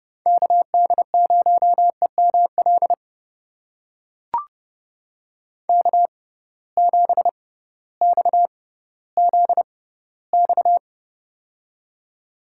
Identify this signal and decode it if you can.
Morse code, heard as a keyed tone, about 30 wpm, spaced as for 16 wpm, 700 Hz. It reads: KD0EML K7XZX
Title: USA Call Signs - 6x Character Spacing - 30wpm